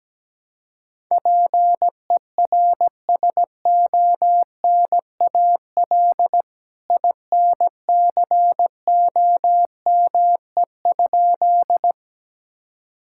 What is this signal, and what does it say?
Morse code, heard as a keyed tone, about 17 wpm, 700 Hz. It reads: PERSONAL INCOME?
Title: Sentences from Top 1000 Words - Review Part 5 - 17wpm